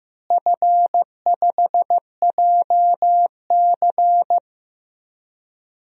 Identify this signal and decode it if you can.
Morse code, heard as a keyed tone, about 15 wpm, 700 Hz. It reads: F5JC
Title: DXCC-100 Call Signs - 4 Characters - Repeated 3x - 15wpm